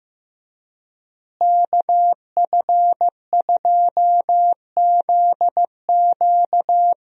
Morse code, heard as a keyed tone, about 16 wpm, 700 Hz. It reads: KF2ZQ
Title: USA Call Signs - 5 Characters - Repeated 3x - 15wpm